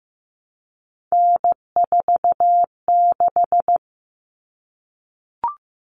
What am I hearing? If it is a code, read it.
Morse code, heard as a keyed tone, about 15 wpm, 700 Hz. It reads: N46
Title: Common Cut Numbers - 3-Digit 15wpm